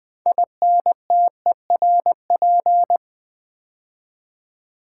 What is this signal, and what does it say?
Morse code, heard as a keyed tone, about 20 wpm, 700 Hz. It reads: INTERP
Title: Build A Word - 3 Letters At A Time - 9-Letter Words - 20wpm